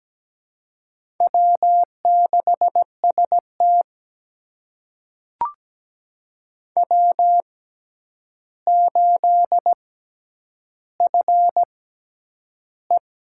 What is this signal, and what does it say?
Morse code, heard as a keyed tone, about 17 wpm, spaced as for 9 wpm, 700 Hz. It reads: W6ST W8FE
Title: USA Call Signs - 4 Characters - 6x Character Spacing - 17wpm